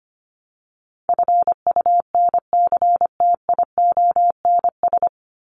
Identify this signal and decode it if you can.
Morse code, heard as a keyed tone, about 25 wpm, 700 Hz. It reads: FUNCTIONS
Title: ES:S Suffix - Top 100 Words with Base Words 25wpm